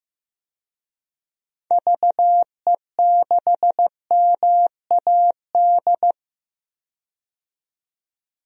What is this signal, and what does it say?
Morse code, heard as a keyed tone, about 15 wpm, 700 Hz. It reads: VE6MAD
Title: DXCC-100 Call Signs - 6 Characters - 15wpm